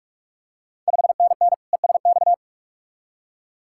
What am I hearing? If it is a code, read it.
Morse code, heard as a keyed tone, about 46 wpm, 690 Hz. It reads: 5NN ESX
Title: Ontario QSO Party Exchange 45wpm